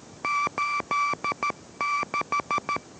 Morse code, heard as a keyed tone, about 13 wpm, 1160 Hz, with noise. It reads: 86